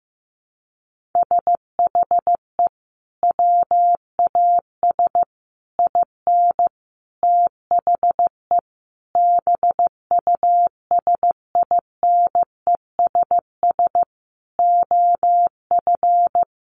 Morse code, heard as a keyed tone, about 15 wpm, 700 Hz. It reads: SHE WAS IN THE BUSINESS OF
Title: Sentences from Top 1500 Words 15wpm